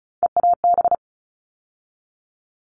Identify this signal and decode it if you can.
Morse code, heard as a keyed tone, about 34 wpm, 700 Hz. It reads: EAB
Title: All Cut Numbers - 3-Digit 35wpm